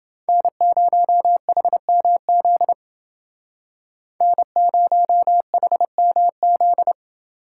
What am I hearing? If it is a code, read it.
Morse code, heard as a keyed tone, about 29 wpm, 700 Hz. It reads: N0HMZ N0HMZ